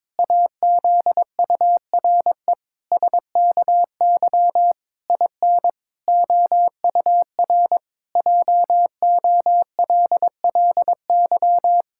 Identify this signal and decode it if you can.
Morse code, heard as a keyed tone, about 22 wpm, 700 Hz. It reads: AZURE SKY IN OUR JOLLY
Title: Pangram Challenge v2 - 22wpm